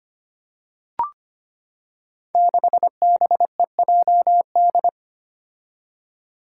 Morse code, heard as a keyed tone, about 25 wpm, 700 Hz. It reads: 6BEJD
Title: Five Challenging Letter-Number Combinations 25wpm